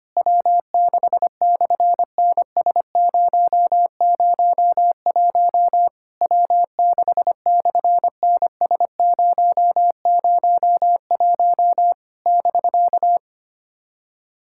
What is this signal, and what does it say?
Morse code, heard as a keyed tone, about 25 wpm, 690 Hz. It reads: W6/NS001 W6/NS001 <BK>
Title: SOTA QSO Practice 25wpm